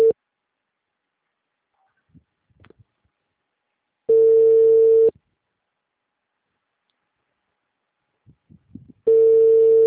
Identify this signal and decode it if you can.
Ringback tone (China, 450 Hz).